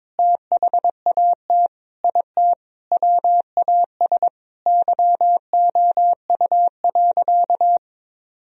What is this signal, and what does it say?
Morse code, heard as a keyed tone, about 22 wpm, 700 Hz. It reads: THAT IT WAS YOU.
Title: Sentences from Top 1000 Words - Review Part 4 - 22wpm